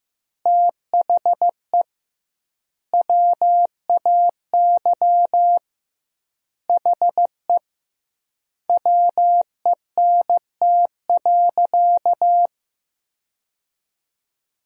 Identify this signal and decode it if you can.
Morse code, heard as a keyed tone, about 15 wpm, 700 Hz. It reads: THE WAY HE WENT.
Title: 2x Word-Spacing - Sentences from Top 200 Words - 15wpm